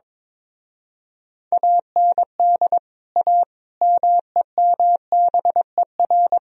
Morse code, heard as a keyed tone, about 22 wpm, 700 Hz. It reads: AND A MEMBER
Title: Speed-Racing - Sets of 5 Words 22wpm